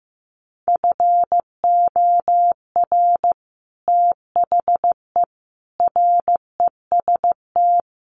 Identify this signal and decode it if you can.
Morse code, heard as a keyed tone, about 15 wpm, 700 Hz. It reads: FOR THE REST
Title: Sets of 5 Words Encore 15wpm